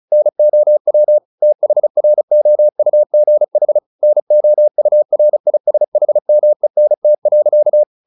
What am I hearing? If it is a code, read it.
Morse code, heard as a keyed tone, about 35 wpm, 600 Hz. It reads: NOW THROUGH NOURISHMENT.
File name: the old man and the sea 2 35wpm